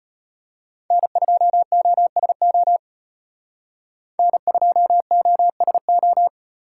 Morse code, heard as a keyed tone, about 35 wpm, 700 Hz. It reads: N2OSO N2OSO